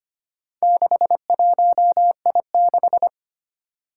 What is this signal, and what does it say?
Morse code, heard as a keyed tone, about 25 wpm, 700 Hz. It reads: 61I6
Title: Four Challenging Letter-Number Combinations 25wpm